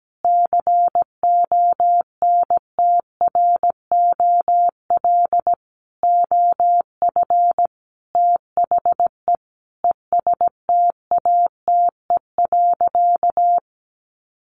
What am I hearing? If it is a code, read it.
Morse code, heard as a keyed tone, about 17 wpm, 700 Hz. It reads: CONTROL OF THE ESTATE.
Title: Sentences from Top 1200 Words 17wpm